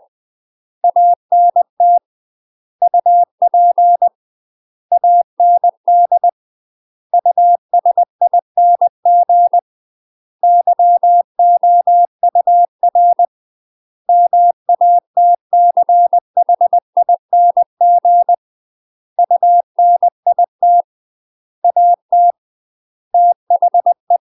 Morse code, heard as a keyed tone, about 20 wpm, 700 Hz. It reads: ANT UP AND USING YOUR MATCHING UNIT AT THE
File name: UK MASTER QSOs TRAINING / 20WPM / UK MASTER QSOs 0026